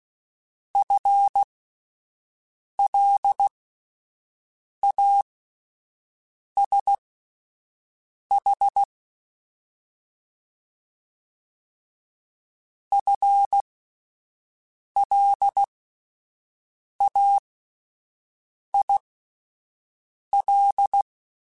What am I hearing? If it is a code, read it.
Morse code, heard as a keyed tone, about 16 wpm, spaced as for 5 wpm, 800 Hz. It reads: FLASH FLAIL
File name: Alphabetical / Alphabetical mp3s / Fs @5